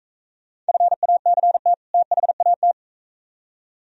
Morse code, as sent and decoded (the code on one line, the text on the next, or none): ..-. .- -.-. - / - .... .- -
FACT THAT